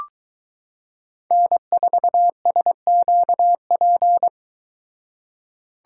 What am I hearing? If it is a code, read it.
Morse code, heard as a keyed tone, about 23 wpm, 700 Hz. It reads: N4SQP